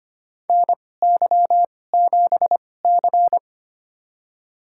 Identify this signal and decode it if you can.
Morse code, heard as a keyed tone, about 25 wpm, spaced as for 18 wpm, 700 Hz. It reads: NY7C
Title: USA Call Signs - 2x Character Spacing - 25wpm